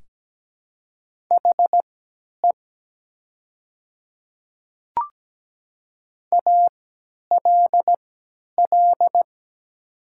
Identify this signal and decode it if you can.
Morse code, heard as a keyed tone, about 17 wpm, spaced as for 10 wpm, 700 Hz. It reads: HE ALL
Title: 3x Character Spacing - Top 100 Words - 17wpm